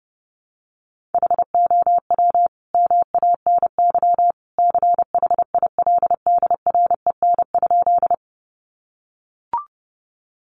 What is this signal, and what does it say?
Morse code, heard as a keyed tone, about 30 wpm, 700 Hz. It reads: HOW MANY CHILDREN?
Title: Sentences from Top 300 Words 30wpm